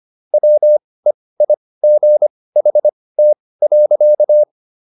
Morse code, heard as a keyed tone, about 25 wpm, spaced as for 18 wpm, 600 Hz. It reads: WEIGHT.
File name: SS108 18